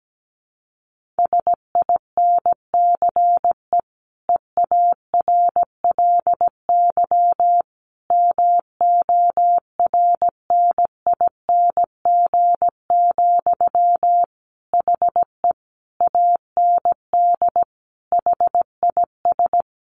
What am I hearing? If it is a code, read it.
Morse code, heard as a keyed tone, about 17 wpm, 700 Hz. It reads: SINCE EARLY MORNING, HE AND HIS
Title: Sentences from Top 1400 Words 17wpm